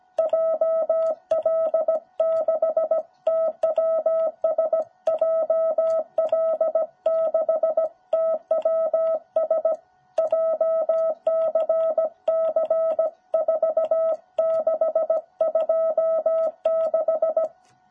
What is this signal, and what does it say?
Morse code, heard as a keyed tone, about 17 wpm, 660 Hz. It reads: JL6TWS JL6TWS JCC4626
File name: jh8cbh / 2021cw